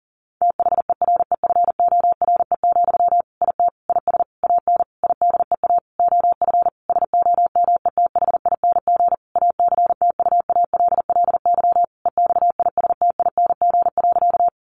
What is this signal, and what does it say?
Morse code, heard as a keyed tone, about 37 wpm, 700 Hz. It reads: THEREFORE, IT IS AN IDEA OF SOMETHING ACTUALLY EXISTING.
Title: Sentences from Top 1500 Words 40wpm